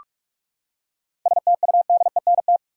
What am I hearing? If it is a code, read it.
Morse code, heard as a keyed tone, about 46 wpm, 700 Hz. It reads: STUDENT
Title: Seven Letter Words - 45wpm